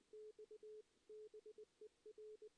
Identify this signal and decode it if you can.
Morse code, heard as a keyed tone, about 21 wpm, 440 Hz. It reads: X BER
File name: 2016 JAN CW NAQP / 2016 Jan naqp k5ot - 0-3